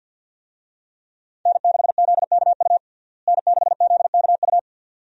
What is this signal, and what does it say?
Morse code, heard as a keyed tone, about 50 wpm, 700 Hz. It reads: N6ZKU N6ZKU
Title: USA Call Signs - 5 Characters - Repeated 2x - 50wpm